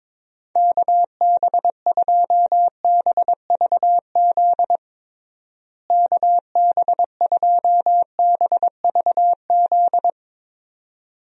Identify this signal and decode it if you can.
Morse code, heard as a keyed tone, about 22 wpm, 700 Hz. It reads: KB2BVZ KB2BVZ
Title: USA Call Signs - 6 Characters - Repeated 2x - 22wpm